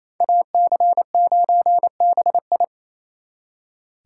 Morse code, heard as a keyed tone, about 35 wpm, 700 Hz. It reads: AC9BI